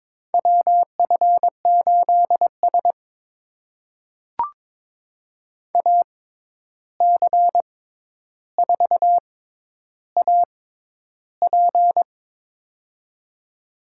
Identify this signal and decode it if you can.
Morse code, heard as a keyed tone, about 22 wpm, spaced as for 11 wpm, 700 Hz. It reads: WF8S AC4AP
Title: USA Call Signs - 6x Character Spacing - 22wpm